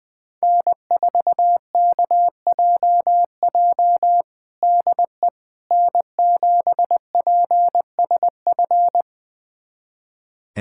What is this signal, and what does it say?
Morse code, heard as a keyed tone, about 21 wpm, 700 Hz. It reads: N4KJJ DE N7PSF
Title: USA Callsign DE Callsign - 5 Characters - 20wpm